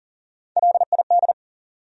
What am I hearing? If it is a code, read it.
Morse code, heard as a keyed tone, about 53 wpm, spaced as for 47 wpm, 700 Hz. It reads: LID
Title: Speed-Racing Abbreviations - 40wpm